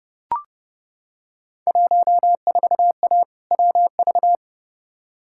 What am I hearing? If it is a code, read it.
Morse code, heard as a keyed tone, about 31 wpm, 700 Hz. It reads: 14A WV
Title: Field Day Exchanges Encore - 30wpm